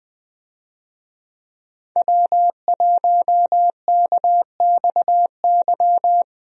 Morse code, heard as a keyed tone, about 20 wpm, 700 Hz. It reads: W1KXY
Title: USA Call Signs - 5 Characters - Repeated 3x - 20wpm